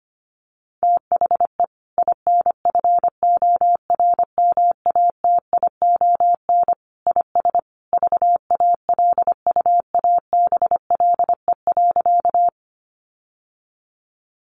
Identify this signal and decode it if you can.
Morse code, heard as a keyed tone, about 25 wpm, 700 Hz. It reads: THE INFORMATION IS VALUABLE.
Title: Sentences from Top 1100 Words 25wpm